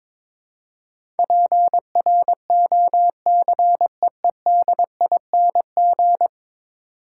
Morse code, heard as a keyed tone, about 22 wpm, 700 Hz. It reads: PROCEEDING
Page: morse-code-ninja-practice-22wpm